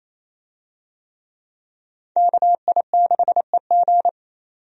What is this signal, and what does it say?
Morse code, heard as a keyed tone, about 27 wpm, 700 Hz. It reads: KI6EG
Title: USA Call Signs - 5 Characters - 28wpm